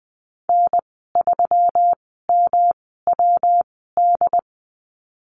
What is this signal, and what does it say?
Morse code, heard as a keyed tone, about 20 wpm, spaced as for 14 wpm, 700 Hz. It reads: N3MWD
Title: 122 Lesson 20g - Number 3 - Call Signs 20wpm